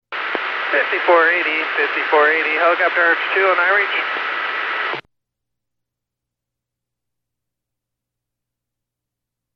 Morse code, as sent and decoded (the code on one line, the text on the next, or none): none